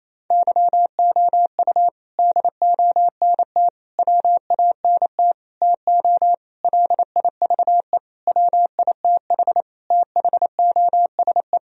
Morse code, heard as a keyed tone, about 28 wpm, 700 Hz. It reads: YOU DONT WANT TO LIVE WITH THOSE
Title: Sentences from Top 1000 Words - Review Part 1 - 28wpm